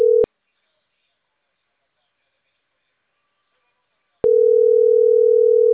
Ringback tone (North America), 440 + 480 Hz.